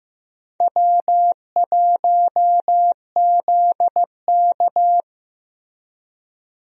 Morse code, heard as a keyed tone, about 14 wpm, 700 Hz. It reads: W1ZK
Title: USA Call Signs - 4 Characters - Repeated 3x - 15wpm